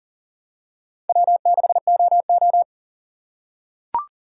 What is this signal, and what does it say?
Morse code, heard as a keyed tone, about 40 wpm, 700 Hz. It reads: W6OO
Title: USA Call Signs - 4 Characters - 3x Character Spacing - 40wpm